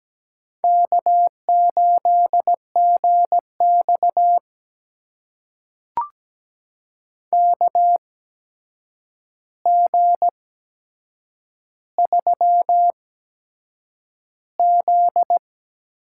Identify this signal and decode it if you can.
Morse code, heard as a keyed tone, about 17 wpm, spaced as for 7 wpm, 700 Hz. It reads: K8GX KG3Z
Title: USA Call Signs - 8x Character Spacing - 17wpm